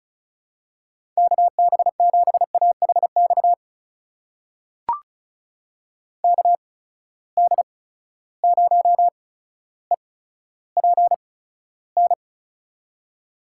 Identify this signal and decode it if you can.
Morse code, heard as a keyed tone, about 34 wpm, spaced as for 15 wpm, 700 Hz. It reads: KB7AHX KD0EPN
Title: USA Call Signs - 6 Characters - 8x Character Spacing - 35wpm